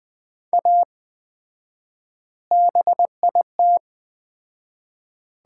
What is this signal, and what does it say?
Morse code, heard as a keyed tone, about 20 wpm, 700 Hz. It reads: A BIT
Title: 4x Word-Spacing - Sets of 4 Words - 20wpm